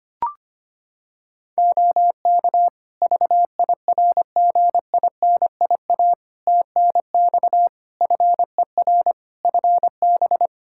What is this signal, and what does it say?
Morse code, heard as a keyed tone, about 25 wpm, 700 Hz. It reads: OK VIRGINIA TNX FER FB
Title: Practice QSO - Exchange 4 - Ending - 25wpm